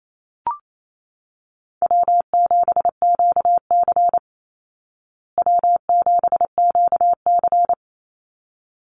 Morse code, heard as a keyed tone, about 28 wpm, 700 Hz. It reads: W7QC W7QC
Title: Call Signs Repeated 3x - 28wpm